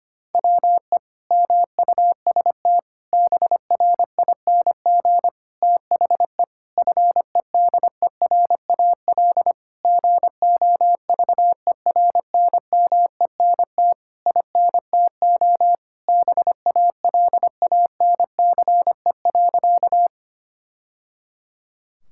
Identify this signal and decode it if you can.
Morse code, heard as a keyed tone, about 25 wpm, 700 Hz. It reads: WE MUST BRING THE FEDERAL GOVERNMENT INTO BALANCE.
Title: Sentences from Top 1000 Words - Review Part 6 - 25wpm